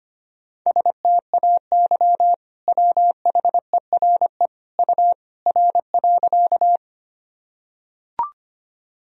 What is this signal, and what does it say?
Morse code, heard as a keyed tone, about 25 wpm, 700 Hz. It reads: STAY WHERE U R.
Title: Sentences from Top 500 Words and Abbreviations 25wpm